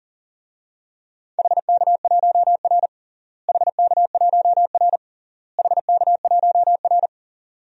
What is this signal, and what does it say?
Morse code, heard as a keyed tone, about 37 wpm, 700 Hz. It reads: HK1R HK1R HK1R
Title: DXCC-100 Call Signs - 4 Characters - Repeated 3x - 40wpm